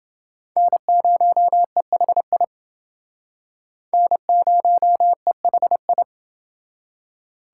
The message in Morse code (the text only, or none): N0EHI N0EHI